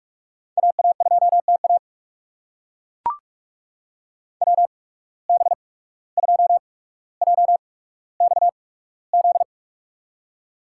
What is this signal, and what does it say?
Morse code, heard as a keyed tone, about 42 wpm, spaced as for 18 wpm, 700 Hz. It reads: AA2TA WB2JXZ
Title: USA Call Signs - 8x Character Spacing - 45wpm